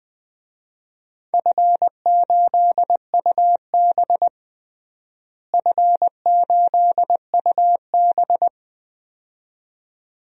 Morse code, heard as a keyed tone, about 20 wpm, 700 Hz. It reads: F8UB F8UB
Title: DXCC-100 Call Signs - 4 Characters - Repeated 2x - 20wpm